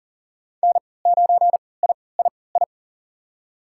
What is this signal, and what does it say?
Morse code, heard as a keyed tone, about 40 wpm, spaced as for 23 wpm, 700 Hz. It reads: N9III
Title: USA Call Signs - 5 Characters - 3x Character Spacing - 40wpm